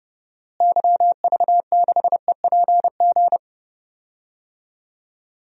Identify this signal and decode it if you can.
Morse code, heard as a keyed tone, about 31 wpm, 700 Hz. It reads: YV6EPG
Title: DXCC-100 Call Signs - 6 Characters - Repeated 2x - 30wpm